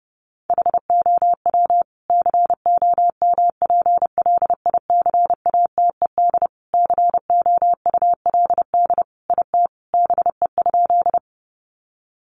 Morse code, heard as a keyed tone, about 30 wpm, 700 Hz. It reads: HOW COMPLICATED COULD IT BE?
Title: Sentences from Top 1200 Words 30wpm